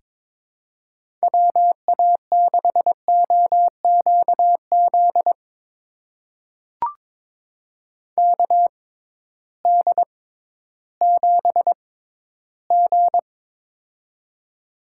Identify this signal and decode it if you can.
Morse code, heard as a keyed tone, about 22 wpm, spaced as for 13 wpm, 700 Hz. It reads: WA6OQZ KD7G
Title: USA Call Signs - 6x Character Spacing - 22wpm